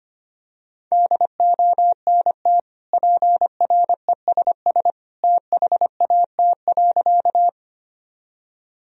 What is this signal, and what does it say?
Morse code, heard as a keyed tone, about 25 wpm, 700 Hz. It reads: DONT PRESS THAT.
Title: Sentences from Top 1000 Words - Review Part 1 - 25wpm